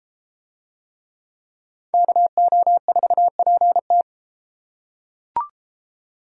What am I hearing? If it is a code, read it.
Morse code, heard as a keyed tone, about 31 wpm, 700 Hz. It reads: KO4PT